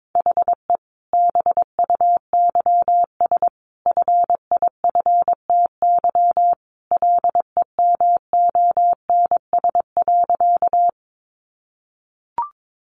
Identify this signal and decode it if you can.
Morse code, heard as a keyed tone, about 22 wpm, 700 Hz. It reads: HE BUYS FIFTY LEMONS.
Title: Additional Easy Phrases 22wpm